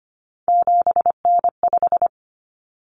Morse code, heard as a keyed tone, about 25 wpm, 700 Hz. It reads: 7N5
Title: Common Cut Numbers - 3-Digit 25wpm